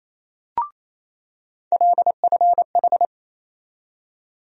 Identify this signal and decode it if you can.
Morse code, heard as a keyed tone, about 28 wpm, 700 Hz. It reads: LFH
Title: USA Call Sign Postfixes - 3 Letters 28wpm